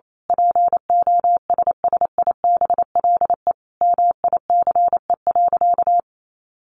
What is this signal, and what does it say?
Morse code, heard as a keyed tone, about 28 wpm, 700 Hz. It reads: POSSIBLE MICE.
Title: A Sound Of Thunder 28wpm